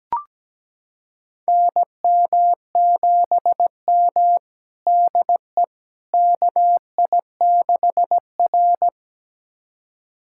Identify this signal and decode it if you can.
Morse code, heard as a keyed tone, about 17 wpm, 700 Hz. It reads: NM7M DE KI6R
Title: USA Callsign DE Callsign - 4 Characters - 17wpm